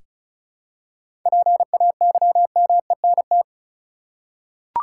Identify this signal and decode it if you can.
Morse code, heard as a keyed tone, about 34 wpm, 700 Hz. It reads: PAYMENT
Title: Seven Letter Words - 35wpm